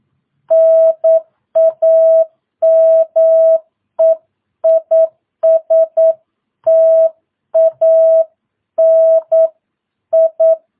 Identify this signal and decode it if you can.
Morse code, heard as a keyed tone, about 9 wpm, 650 Hz. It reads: NAME IS TANI
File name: jh8cbh / 2022cw